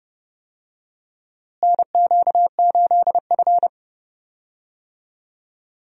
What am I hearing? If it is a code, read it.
Morse code, heard as a keyed tone, about 32 wpm, 700 Hz. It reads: NQ8F
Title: USA Call Signs - 4 Characters - Repeated 3x - 30wpm